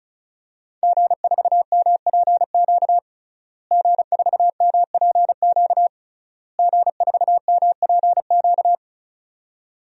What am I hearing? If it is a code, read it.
Morse code, heard as a keyed tone, about 36 wpm, 700 Hz. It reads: G4MPQ G4MPQ G4MPQ
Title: DXCC-100 Call Signs - 5 Characters - Repeated 3x - 35wpm